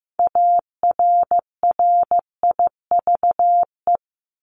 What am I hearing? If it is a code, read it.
Morse code, heard as a keyed tone, about 15 wpm, 700 Hz. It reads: ARRIVE
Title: ED Suffix - Top 100 Words with Base Words 15wpm